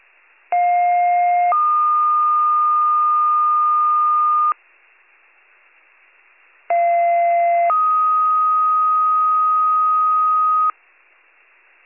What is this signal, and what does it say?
Two-tone page, heard as a one-second tone, then a three-second tone, about 710 then 1220 Hz, about 690 then 1250 Hz. It is heard from 0.5 to 10.7 s.